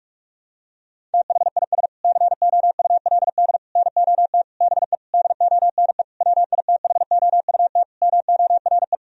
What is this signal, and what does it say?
Morse code, heard as a keyed tone, about 41 wpm, 700 Hz. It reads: THIS COULD NOT BE DONE WITHOUT MORE
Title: Sentences from Top 1000 Words - Review Part 3 - 45wpm